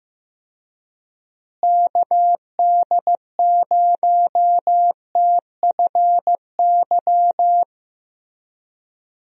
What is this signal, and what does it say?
Morse code, heard as a keyed tone, about 16 wpm, 700 Hz. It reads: KD0TFY
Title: USA Call Signs - 6 Characters - Repeated 2x - 15wpm